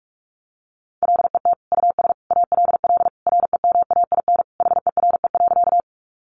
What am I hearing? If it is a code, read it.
Morse code, heard as a keyed tone, about 44 wpm, 700 Hz. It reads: LET US ALL REMAIN HERE.
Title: Sentences from Top 1200 Words 45wpm